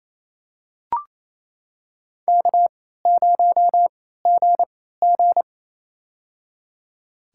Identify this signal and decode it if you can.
Morse code, heard as a keyed tone, about 28 wpm, spaced as for 16 wpm, 700 Hz. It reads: K0GG
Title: USA Call Signs - 4 Characters - 3x Character Spacing - 28wpm